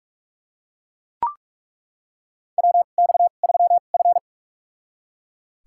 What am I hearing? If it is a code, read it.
Morse code, heard as a keyed tone, about 41 wpm, spaced as for 31 wpm, 700 Hz. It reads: WX3F
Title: USA Call Signs - 2x Character Spacing - 45wpm